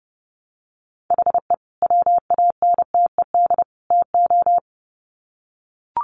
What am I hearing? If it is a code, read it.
Morse code, heard as a keyed tone, about 30 wpm, 700 Hz. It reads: HE WANTED TO
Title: Sets of 3 Words 30wpm